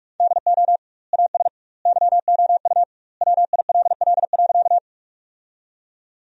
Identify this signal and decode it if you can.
Morse code, heard as a keyed tone, about 41 wpm, 700 Hz. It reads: DO AS YOU WILL.
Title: 2x Word-Spacing - Sentences from Top 100 Words - 45wpm